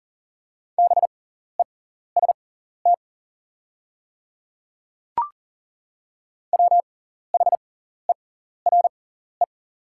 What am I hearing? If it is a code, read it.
Morse code, heard as a keyed tone, about 40 wpm, spaced as for 14 wpm, 700 Hz. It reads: BEST WHERE
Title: 6x Character Spacing - Top 500 Words - 40wpm